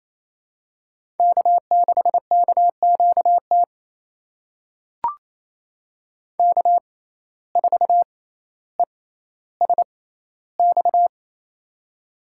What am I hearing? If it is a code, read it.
Morse code, heard as a keyed tone, about 28 wpm, spaced as for 14 wpm, 700 Hz. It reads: K6KQT K4ESX
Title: USA Call Signs - 5 Characters - 6x Character Spacing - 28wpm